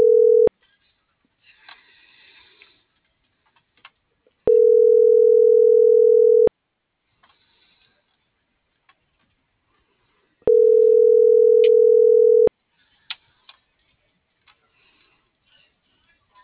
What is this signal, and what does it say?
Ringback tone (North America), 440 + 480 Hz.